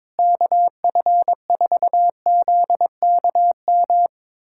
Morse code, heard as a keyed tone, about 21 wpm, 700 Hz. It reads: KF4ZKM
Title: USA Callsign DE Callsign 22wpm